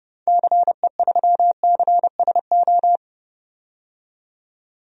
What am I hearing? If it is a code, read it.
Morse code, heard as a keyed tone, about 29 wpm, 700 Hz. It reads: CE3CSO
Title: DXCC-100 Call Signs - 6 Characters - 30wpm